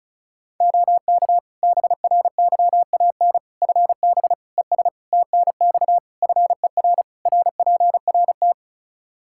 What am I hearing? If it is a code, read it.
Morse code, heard as a keyed tone, about 34 wpm, 700 Hz. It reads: OK BRYAN FB ES TNX FER RPRT
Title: Practice QSO - Exchange 2 - Info - 35wpm